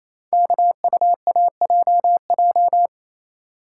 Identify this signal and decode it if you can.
Morse code, heard as a keyed tone, about 28 wpm, 700 Hz. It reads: KUAJJ
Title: Code Groups - 5 Letters - 28wpm